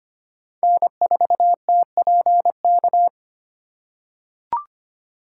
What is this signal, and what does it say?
Morse code, heard as a keyed tone, about 25 wpm, 700 Hz. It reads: N4TPK
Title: USA Call Signs - 3x Character Spacing - 25wpm